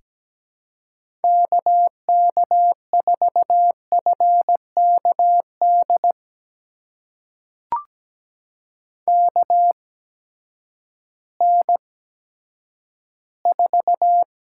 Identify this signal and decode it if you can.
Morse code, heard as a keyed tone, about 17 wpm, spaced as for 10 wpm, 700 Hz. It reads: KK4FKD KN4
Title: USA Call Signs - 6 Characters - 8x Character Spacing - 17wpm